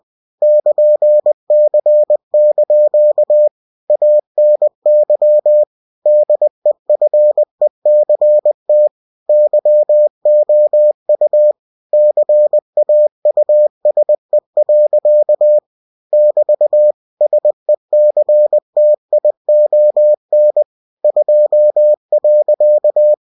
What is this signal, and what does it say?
Morse code, heard as a keyed tone, about 20 wpm, 600 Hz. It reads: <KN>C) ANY DEFECT YOU CAUSE. <BT> SECTION 2.
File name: cw-books / winnie-the-pooh / winnie-20wpm-page-0120